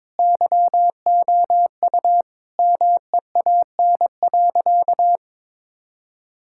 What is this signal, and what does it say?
Morse code, heard as a keyed tone, about 22 wpm, 700 Hz. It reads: YOU MEAN.
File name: practice / 22wpm / Sentences with 2-Word Phrases - 22wpm